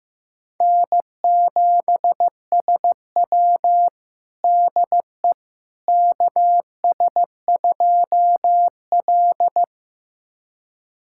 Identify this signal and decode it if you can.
Morse code, heard as a keyed tone, about 15 wpm, 700 Hz. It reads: N7SW DE KS2L
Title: USA Callsign DE Callsign - 4 Characters - 15wpm